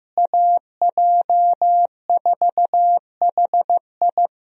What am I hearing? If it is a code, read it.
Morse code, heard as a keyed tone, about 15 wpm, 700 Hz. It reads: AJ4HI
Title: USA Call Signs - 5 Characters - Repeated 3x - 15wpm